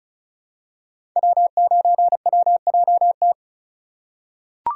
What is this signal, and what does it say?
Morse code, heard as a keyed tone, about 36 wpm, 690 Hz. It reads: W9WJT
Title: Code Groups - 5 Letter:Numbers - 35wpm